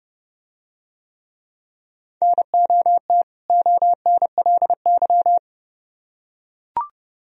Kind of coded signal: Morse code

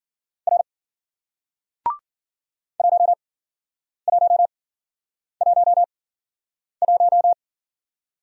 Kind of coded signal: Morse code

